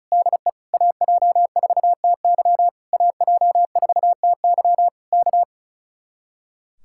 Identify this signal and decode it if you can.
Morse code, heard as a keyed tone, about 33 wpm, 700 Hz. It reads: DE AJ4TY AJ4TY K
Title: Speed-Racing Common QSO Phrases Encore - 35wpm